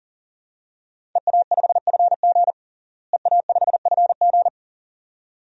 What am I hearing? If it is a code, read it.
Morse code, heard as a keyed tone, about 40 wpm, 700 Hz. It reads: EA5FG EA5FG
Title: DXCC-100 Call Signs - 5 Characters - Repeated 2x - 40wpm